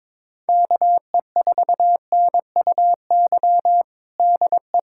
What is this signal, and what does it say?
Morse code, heard as a keyed tone, about 22 wpm, 700 Hz. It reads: KE4NUY DE
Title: USA Callsign DE Callsign 22wpm